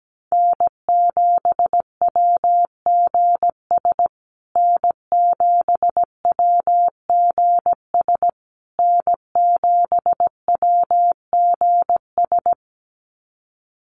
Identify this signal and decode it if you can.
Morse code, heard as a keyed tone, about 17 wpm, 700 Hz. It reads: N7WGS N7WGS N7WGS
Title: USA Call Signs Repeated 3x 17wpm